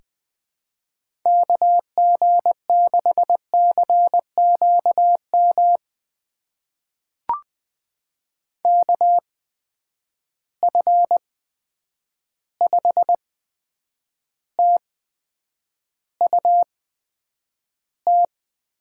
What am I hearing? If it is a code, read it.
Morse code, heard as a keyed tone, about 20 wpm, spaced as for 9 wpm, 700 Hz. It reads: KG6CQM KF5TUT
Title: USA Call Signs - 8x Character Spacing - 20wpm